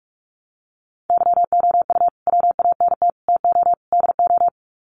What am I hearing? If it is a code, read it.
Morse code, heard as a keyed tone, about 44 wpm, 700 Hz. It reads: YOU WANT TO DO
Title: Sets of 4 Words 45wpm